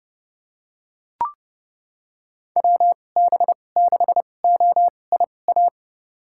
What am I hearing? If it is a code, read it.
Morse code, heard as a keyed tone, about 32 wpm, spaced as for 22 wpm, 700 Hz. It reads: WB6OIA
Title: USA Call Signs - 6 Characters - 2x Character Spacing - 30wpm